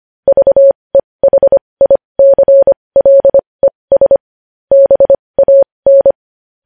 Morse code, heard as a keyed tone, about 25 wpm, spaced as for 20 wpm, 550 Hz. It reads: VEHICLES BAN
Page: news headlines in morse code at 20 wpm